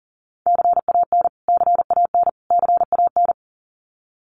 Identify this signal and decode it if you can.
Morse code, heard as a keyed tone, about 40 wpm, 700 Hz. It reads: CAN CAN CAN
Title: Top 500 Words Repeated 40wpm